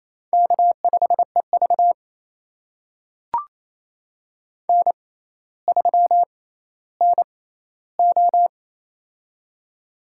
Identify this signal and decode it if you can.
Morse code, heard as a keyed tone, about 27 wpm, spaced as for 14 wpm, 700 Hz. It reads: K5EV N3NO
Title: USA Call Signs - 4 Characters - 6x Character Spacing - 28wpm